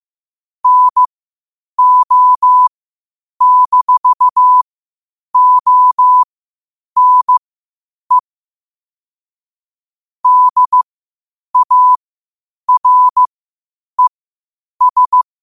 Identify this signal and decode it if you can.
Morse code, heard as a keyed tone, about 15 wpm, spaced as for 8 wpm, 1000 Hz. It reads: NO-ONE DARES
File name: qotdfiles / 8wpm